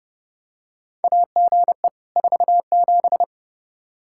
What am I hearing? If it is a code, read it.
Morse code, heard as a keyed tone, about 30 wpm, 700 Hz. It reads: AGE 47
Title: K1USN SST QSO Practice 30wpm